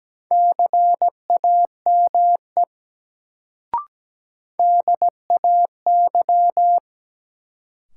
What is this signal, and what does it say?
Morse code, heard as a keyed tone, about 17 wpm, 700 Hz. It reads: CAME DAY
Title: Sentences from Top 200 Words - New Word Drill - 17wpm